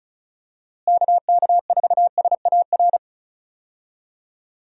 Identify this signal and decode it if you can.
Morse code, heard as a keyed tone, about 34 wpm, 700 Hz. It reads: KK4SAR
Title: USA Call Signs - 6 Characters - Repeated 3x - 35wpm